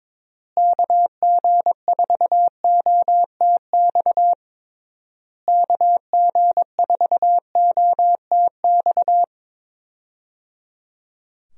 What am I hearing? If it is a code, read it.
Morse code, heard as a keyed tone, about 21 wpm, 700 Hz. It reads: KG4OTX KG4OTX
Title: USA Call Signs - 6 Characters - Repeated 3x - 22wpm